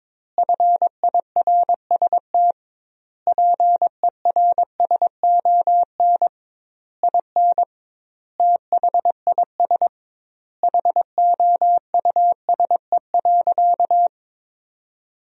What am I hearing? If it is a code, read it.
Morse code, heard as a keyed tone, about 22 wpm, 700 Hz. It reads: FIRST PERSON IN THIS HOUSE.
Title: 2x Word-Spacing - Sentences from Top 400 Words - 22wpm